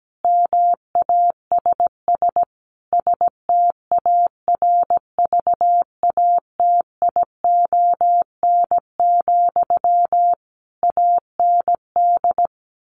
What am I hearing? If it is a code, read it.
Morse code, heard as a keyed tone, about 17 wpm, 700 Hz. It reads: MASS STARVATION, AND
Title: A Sound Of Thunder 17wpm